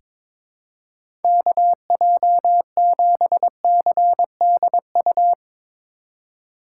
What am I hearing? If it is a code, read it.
Morse code, heard as a keyed tone, about 21 wpm, 700 Hz. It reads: KJ7CDU
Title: USA Call Signs - 6 Characters - 4x Character Spacing - 22wpm